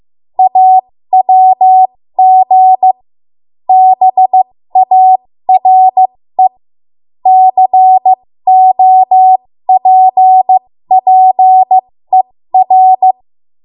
Morse code, heard as a keyed tone, about 15 wpm, spaced as for 13 wpm, 750 Hz. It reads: AWG BARE COPPER
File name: Morse / 13WPM